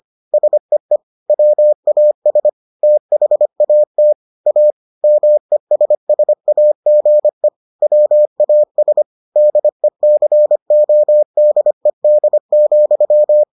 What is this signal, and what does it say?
Morse code, heard as a keyed tone, about 25 wpm, 600 Hz. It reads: SEE WAS THAT A MESSAGE WAS DECODED,